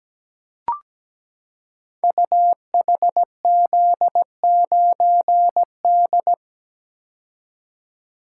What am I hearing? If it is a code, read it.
Morse code, heard as a keyed tone, about 17 wpm, 700 Hz. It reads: UHZ9D
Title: Code Groups - 5 Letter:Numbers - 17wpm